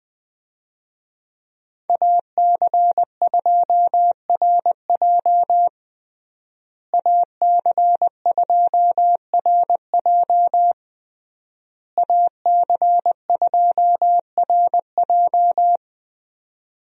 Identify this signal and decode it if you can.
Morse code, heard as a keyed tone, about 20 wpm, 700 Hz. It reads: AC2RJ AC2RJ AC2RJ
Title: USA Call Signs - 5 Characters - Repeated 3x - 20wpm